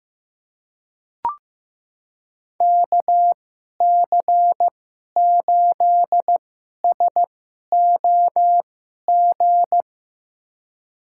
Morse code, heard as a keyed tone, about 15 wpm, spaced as for 11 wpm, 700 Hz. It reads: KC8SOG
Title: USA Call Signs - 2x Character Spacing - 15wpm